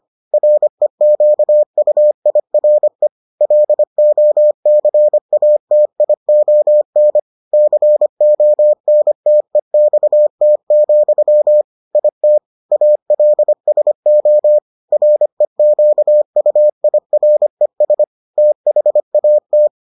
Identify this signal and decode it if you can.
Morse code, heard as a keyed tone, about 25 wpm, 600 Hz. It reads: REQUIRE LOCATION CONTEXT, IT ALSO REQUIRES THAT